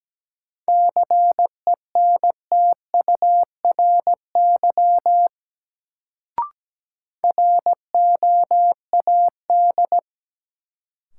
Morse code, heard as a keyed tone, about 17 wpm, 700 Hz. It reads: CENTURY ROAD
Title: Sets of 5 Words - New Word Drill - 17wpm